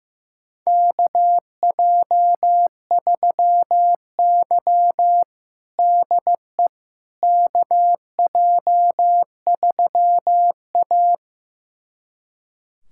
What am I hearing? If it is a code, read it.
Morse code, heard as a keyed tone, about 15 wpm, 700 Hz. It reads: KJ3Y DE KJ3A
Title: USA Callsign DE Callsign - 4 Characters - 15wpm